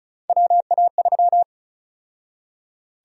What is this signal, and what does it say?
Morse code, heard as a keyed tone, about 36 wpm, 700 Hz. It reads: WA3
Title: USA Call Sign Prefixes - 35wpm